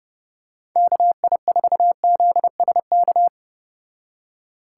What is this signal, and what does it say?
Morse code, heard as a keyed tone, about 30 wpm, 700 Hz. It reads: KI4ZSK